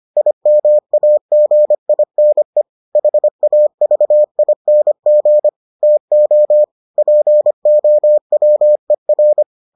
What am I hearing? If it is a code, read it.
Morse code, heard as a keyed tone, about 25 wpm, 600 Hz. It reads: IMAGINE HAVING TO POWER